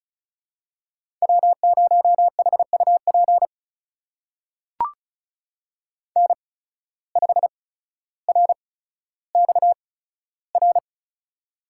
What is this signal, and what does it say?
Morse code, heard as a keyed tone, about 33 wpm, spaced as for 15 wpm, 700 Hz. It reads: W0HUP N5RXR
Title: USA Call Signs - 5 Characters - 8x Character Spacing - 35wpm